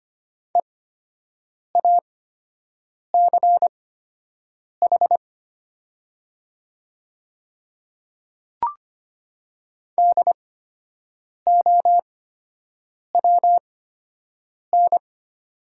Morse code, heard as a keyed tone, about 25 wpm, spaced as for 7 wpm, 700 Hz. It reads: EACH DOWN
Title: 8x Character Spacing - Top 100 Words - 25wpm